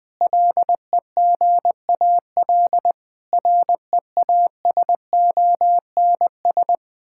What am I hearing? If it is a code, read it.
Morse code, heard as a keyed tone, about 19 wpm, 700 Hz. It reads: LEGAL REASONS
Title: Sentences with 8-Word Phrases - Spoken Beginning - 20wpm